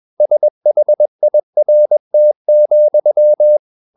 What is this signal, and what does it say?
Morse code, heard as a keyed tone, about 21 wpm, 600 Hz. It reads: SHIRT,
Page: CW med Gnister